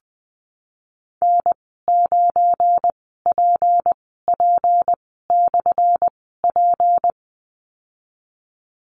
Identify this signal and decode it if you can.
Morse code, heard as a keyed tone, about 20 wpm, spaced as for 15 wpm, 700 Hz. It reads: N9PP/P
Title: 179 Lesson 29g - Slash - Call Signs 20wpm